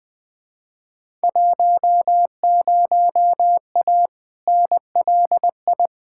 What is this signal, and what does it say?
Morse code, heard as a keyed tone, about 20 wpm, 700 Hz. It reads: 10A NLI
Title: Field Day Exchanges Encore - 20wpm